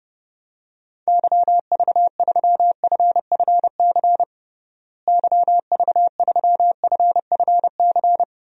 Morse code, heard as a keyed tone, about 30 wpm, 700 Hz. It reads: YV3FFC YV3FFC
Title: DXCC-100 Call Signs - 6 Characters - Repeated 3x - 30wpm